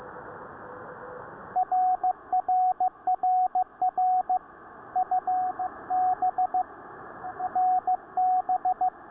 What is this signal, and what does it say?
Morse code, heard as a keyed tone, about 15 wpm, 740 Hz, with radio noise. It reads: RRRR U B NB